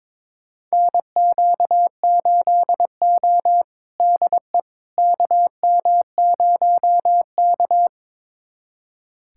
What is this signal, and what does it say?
Morse code, heard as a keyed tone, about 21 wpm, 700 Hz. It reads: NQ8O DE KM0K
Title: USA Callsign DE Callsign - 4 Characters - 22wpm